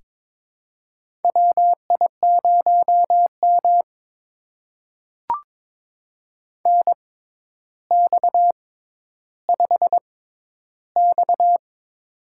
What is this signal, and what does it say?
Morse code, heard as a keyed tone, about 22 wpm, spaced as for 11 wpm, 700 Hz. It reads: WI0M NX5X
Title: USA Call Signs - 4 Characters - 6x Character Spacing - 22wpm